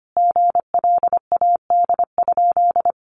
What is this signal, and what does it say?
Morse code, heard as a keyed tone, about 25 wpm, 700 Hz. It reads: GLAD?
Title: Sentences from Top 800 Words 25wpm